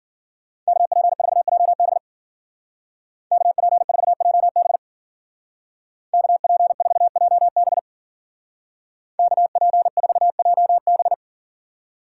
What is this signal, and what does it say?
Morse code, heard as a keyed tone, about 50 wpm, 700 Hz. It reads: KP4JB KP4JB KP4JB KP4JB